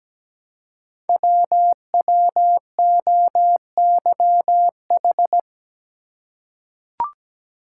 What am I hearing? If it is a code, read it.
Morse code, heard as a keyed tone, about 17 wpm, 700 Hz. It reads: WWOYH
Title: Code Groups - 5 Letters - 17wpm